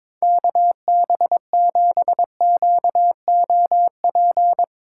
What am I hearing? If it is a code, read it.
Morse code, heard as a keyed tone, about 22 wpm, 700 Hz. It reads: KB7QOP
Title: K1USN SST QSO Practice - 22wpm